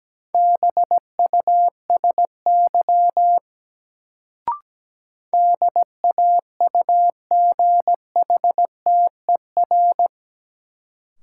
Morse code, heard as a keyed tone, about 17 wpm, 700 Hz. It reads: BUSY DAUGHTER
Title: Sentences from Top 1500 Words - New Word Drill Alt - 17wpm